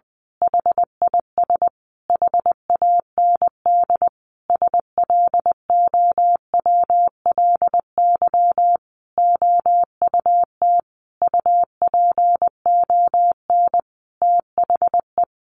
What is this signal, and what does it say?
Morse code, heard as a keyed tone, about 20 wpm, 700 Hz. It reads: HIS HAND SLOWLY OUT UPON THE
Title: A Sound Of Thunder 20wpm